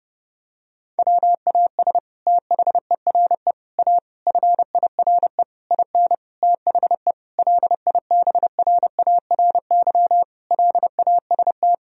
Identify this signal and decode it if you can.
Morse code, heard as a keyed tone, about 32 wpm, 700 Hz. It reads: WAS THERE A FIRE IN THE LIBRARY LAST
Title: Sentences from Top 1000 Words - Review Part 4 - 30wpm